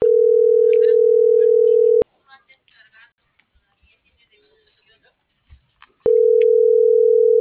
Ringback tone (North America), 440 + 480 Hz.